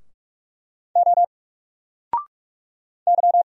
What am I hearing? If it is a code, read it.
Morse code, heard as a keyed tone, about 44 wpm, 700 Hz. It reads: O Y
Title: Single Letter-Number Alt 45wpm